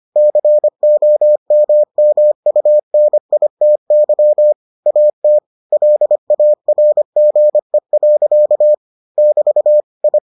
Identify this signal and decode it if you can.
Morse code, heard as a keyed tone, about 25 wpm, 600 Hz. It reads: COMMUNITY AT LARGE. <BT> I